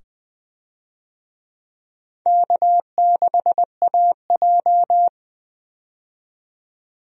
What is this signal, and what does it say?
Morse code, heard as a keyed tone, about 21 wpm, 700 Hz. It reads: K6AJ
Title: USA Call Signs - 4 Characters - 20wpm